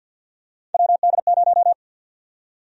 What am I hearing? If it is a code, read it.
Morse code, heard as a keyed tone, about 51 wpm, 700 Hz. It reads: WD0
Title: USA Call Sign Prefixes - Repeated 2x - 50wpm